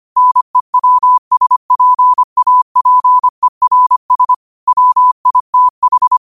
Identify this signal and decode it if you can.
Morse code, heard as a keyed tone, about 25 wpm, 1000 Hz. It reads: NEWSPAPERS WITH
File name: qotdfiles / 25wpm